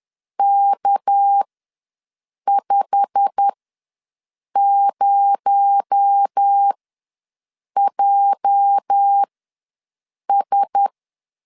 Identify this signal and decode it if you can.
Morse code, heard as a keyed tone, about 10 wpm, spaced as for 6 wpm, 800 Hz. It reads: K50JS
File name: LEZIONI / 8 YJ45 / 8 YJ45 ws 10wpm 3843